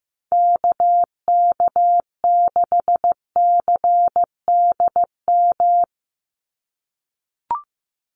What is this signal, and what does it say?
Morse code, heard as a keyed tone, about 15 wpm, 700 Hz. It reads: KK6CDM
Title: USA Call Signs 15wpm